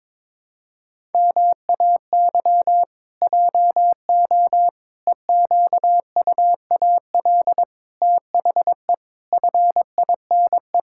Morse code, heard as a keyed tone, about 22 wpm, 700 Hz. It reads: MAY JO EQUAL THE FINE
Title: Pangram Challenge v2 - 22wpm